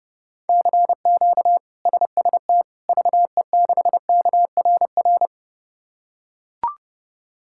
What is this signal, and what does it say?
Morse code, heard as a keyed tone, about 32 wpm, 700 Hz. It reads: CQ SST VE6KRR
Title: K1USN SST QSO Practice - 30wpm